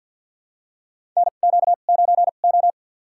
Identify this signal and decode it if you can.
Morse code, heard as a keyed tone, about 52 wpm, spaced as for 36 wpm, 700 Hz. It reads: NQ9O
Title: USA Call Signs - 4 Characters - 2x Character Spacing - 50wpm